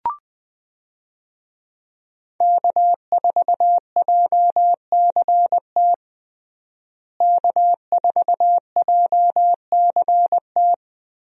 Morse code, heard as a keyed tone, about 21 wpm, 700 Hz. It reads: K4JCT K4JCT
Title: USA Call Signs - 5 Characters - Repeated 3x - 20wpm